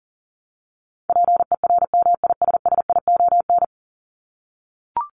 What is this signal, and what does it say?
Morse code, heard as a keyed tone, about 40 wpm, 700 Hz. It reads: PERMISSION
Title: Top 1501-1750 Words 40wpm